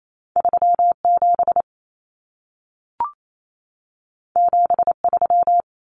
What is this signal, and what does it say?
Morse code, heard as a keyed tone, about 28 wpm, 700 Hz. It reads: 37 73
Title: Common Cut Numbers - 2-Digit 28wpm